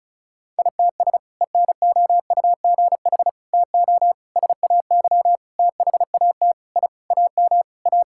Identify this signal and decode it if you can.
Morse code, heard as a keyed tone, about 36 wpm, 700 Hz. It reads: ITS ENOUGH TO SAY THAT I AM A
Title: Sentences from Top 1000 Words - Review Part 6 - 35wpm